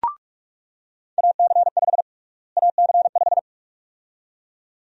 Morse code, heard as a keyed tone, about 46 wpm, 700 Hz. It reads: AC5 AC5
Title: USA Call Sign Prefixes - Repeated 2x - 45wpm